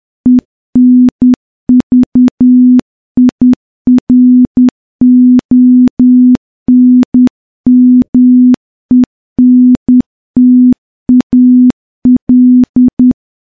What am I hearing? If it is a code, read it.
Morse code, heard as a keyed tone, about 10 wpm, 250 Hz. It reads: ENVIRONMENTAL
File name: morse-training-10WPM